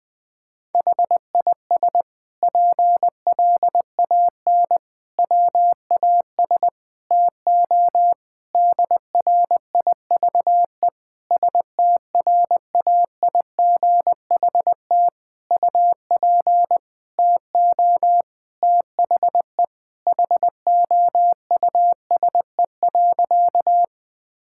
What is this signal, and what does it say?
Morse code, heard as a keyed tone, about 20 wpm, 700 Hz. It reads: HIS PLAN WAS TO DRIVE STRAIGHT UP TO THE HOUSE.
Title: Sentences from Top 1000 Words - Review Part 6 - 20wpm